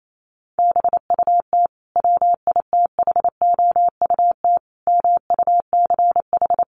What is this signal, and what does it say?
Morse code, heard as a keyed tone, about 28 wpm, 700 Hz. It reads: BUT WITHOUT MUCH
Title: Sentences from Top 900 Words 28wpm